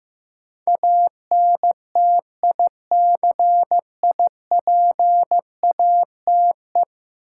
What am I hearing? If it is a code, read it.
Morse code, heard as a keyed tone, about 15 wpm, 700 Hz. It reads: ANTICIPATE
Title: Sentences from Top 1500 Words - New Word Drill - 15wpm